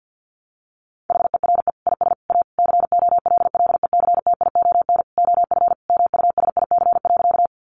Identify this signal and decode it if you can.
Morse code, heard as a keyed tone, about 50 wpm, 700 Hz. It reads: HERE IS A COLLECTION OF MUSIC.
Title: Sentences from Top 1200 Words 50wpm